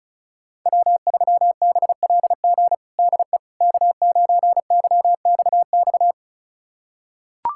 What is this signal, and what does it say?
Morse code, heard as a keyed tone, about 36 wpm, 690 Hz. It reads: W3BLG DE K9YXX
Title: USA Callsign DE Callsign - 5 Characters - 35wpm